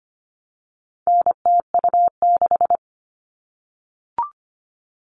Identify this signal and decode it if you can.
Morse code, heard as a keyed tone, about 25 wpm, 700 Hz. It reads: NTU6
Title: All Cut Numbers - 4-Digit 25wpm